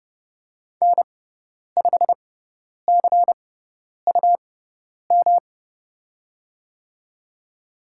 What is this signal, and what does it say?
Morse code, heard as a keyed tone, about 30 wpm, spaced as for 10 wpm, 700 Hz. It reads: N5CUM
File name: Lesson 18g - Letter m - Call Signs 30wpm